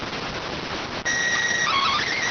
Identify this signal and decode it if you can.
SSTV (slow-scan television), PD 290.